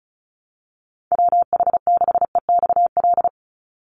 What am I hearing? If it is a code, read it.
Morse code, heard as a keyed tone, about 34 wpm, 700 Hz. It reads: WH6EXL
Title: USA Call Signs 35wpm